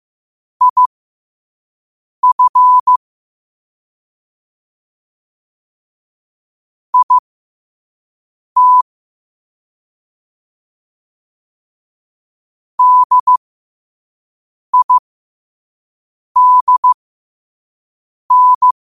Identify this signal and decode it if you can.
Morse code, heard as a keyed tone, about 15 wpm, spaced as for 5 wpm, 1000 Hz. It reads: IF IT DIDN
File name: qotdfiles / 5wpm